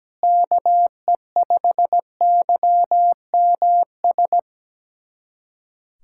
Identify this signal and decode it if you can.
Morse code, heard as a keyed tone, about 17 wpm, 700 Hz. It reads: KE5YMS
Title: USA Callsign DE Callsign - 6 Characters - 17wpm